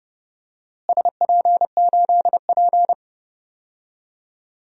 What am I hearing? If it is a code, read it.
Morse code, heard as a keyed tone, about 30 wpm, 700 Hz. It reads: SP8P
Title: DXCC-100 Call Signs - 4 Characters - Repeated 2x - 30wpm